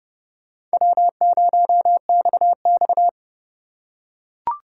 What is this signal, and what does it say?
Morse code, heard as a keyed tone, about 27 wpm, 700 Hz. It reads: W0XX